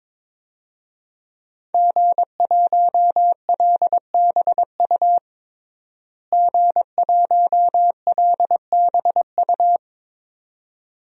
Morse code, heard as a keyed tone, about 22 wpm, 700 Hz. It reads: G1LBU G1LBU
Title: DXCC-100 Call Signs - 5 Characters - Repeated 2x - 22wpm